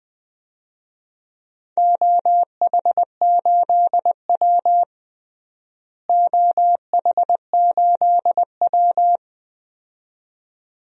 Morse code, heard as a keyed tone, about 21 wpm, 700 Hz. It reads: OH8W OH8W
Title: DXCC-100 Call Signs - 4 Characters - Repeated 2x - 20wpm